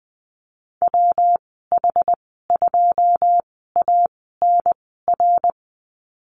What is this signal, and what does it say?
Morse code, heard as a keyed tone, about 19 wpm, spaced as for 14 wpm, 700 Hz. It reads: WH2ANR
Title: 122 Lesson 20g - Number 3 - Call Signs 20wpm